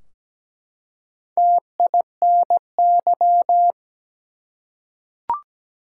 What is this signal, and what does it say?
Morse code, heard as a keyed tone, about 17 wpm, 700 Hz. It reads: TINY
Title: Four Letter Words - 17wpm